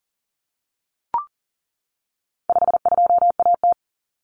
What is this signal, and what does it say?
Morse code, heard as a keyed tone, about 40 wpm, 700 Hz. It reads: 52AT
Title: Common Cut Numbers - 4-Digit 40wpm